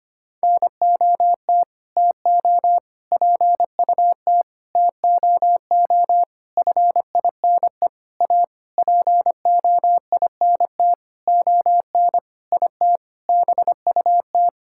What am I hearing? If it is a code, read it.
Morse code, heard as a keyed tone, about 25 wpm, 700 Hz. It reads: NOT TO PUT TOO FINE A POINT ON IT BUT
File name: practice / 25wpm / Sentences with 9-Word Phrases - 25wpm